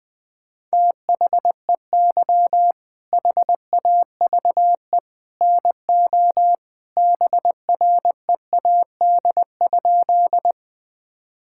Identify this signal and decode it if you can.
Morse code, heard as a keyed tone, about 20 wpm, 700 Hz. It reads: THEY HAVE NO BREAD?
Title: Sentences from Top 1600 Words 20wpm